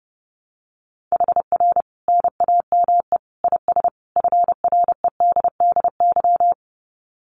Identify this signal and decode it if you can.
Morse code, heard as a keyed tone, about 28 wpm, 700 Hz. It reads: HR NAME IS FREDDY
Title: Common QSO Phrases 30wpm